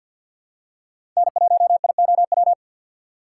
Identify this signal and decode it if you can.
Morse code, heard as a keyed tone, about 48 wpm, 700 Hz. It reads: N1IOW
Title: USA Call Signs - 6x Character Spacing - 50wpm